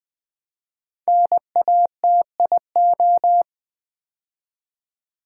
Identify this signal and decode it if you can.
Morse code, heard as a keyed tone, about 20 wpm, 700 Hz. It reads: NATIO
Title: Build A Word - 5 Letters At A Time - 10-Letter Words - 20wpm